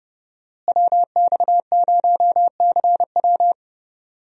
Morse code, heard as a keyed tone, about 30 wpm, 700 Hz. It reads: WX0CW
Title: USA Call Signs - 5 Characters - Repeated 3x - 30wpm